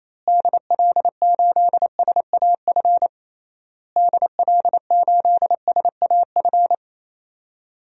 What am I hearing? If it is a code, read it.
Morse code, heard as a keyed tone, about 27 wpm, 700 Hz. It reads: DL8SAF DL8SAF
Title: DXCC-100 Call Signs - 6 Characters - Repeated 3x - 28wpm